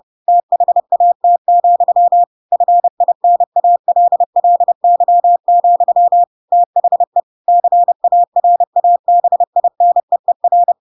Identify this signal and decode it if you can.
Morse code, heard as a keyed tone, about 30 wpm, 700 Hz. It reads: THAT, FINALLY, THE CARABINEER
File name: Pinocchio / Pinocchio ch3 30 30 ik1hge